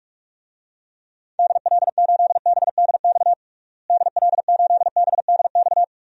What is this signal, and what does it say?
Morse code, heard as a keyed tone, about 41 wpm, 700 Hz. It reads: DL8BDX DL8BDX
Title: DXCC-100 Call Signs - 6 Characters - Repeated 3x - 45wpm